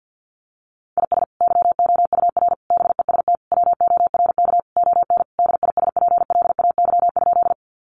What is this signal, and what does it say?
Morse code, heard as a keyed tone, about 50 wpm, 700 Hz. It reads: IS YOUR BEST WORK ON DISPLAY?
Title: Sentences from Top 1200 Words 50wpm